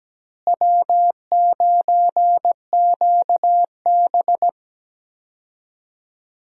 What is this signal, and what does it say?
Morse code, heard as a keyed tone, about 17 wpm, 700 Hz. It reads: W9QB
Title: USA Call Signs - 4 Characters - Repeated 2x - 17wpm